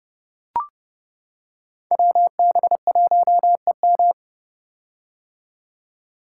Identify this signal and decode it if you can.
Morse code, heard as a keyed tone, about 29 wpm, 700 Hz. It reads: WB1EM
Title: Code Groups - 5 Letter:Numbers - 30wpm